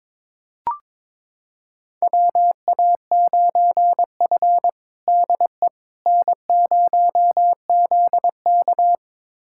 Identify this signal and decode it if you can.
Morse code, heard as a keyed tone, about 22 wpm, 700 Hz. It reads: WA9F DE N0ZK
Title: USA Callsign DE Callsign - 4 Characters - 22wpm